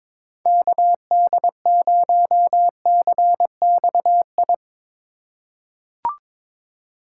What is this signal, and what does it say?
Morse code, heard as a keyed tone, about 22 wpm, 700 Hz. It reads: KD0CXI
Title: USA Call Signs - 2x Character Spacing - 22wpm